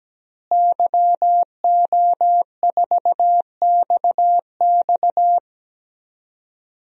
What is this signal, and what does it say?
Morse code, heard as a keyed tone, about 17 wpm, 700 Hz. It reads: YO4XX
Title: DXCC-100 Call Signs - 5 Characters - Repeated 3x - 17wpm